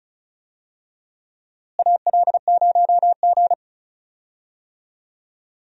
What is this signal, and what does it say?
Morse code, heard as a keyed tone, about 36 wpm, 700 Hz. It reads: AL0G
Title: USA Call Signs - 4 Characters - Repeated 2x - 35wpm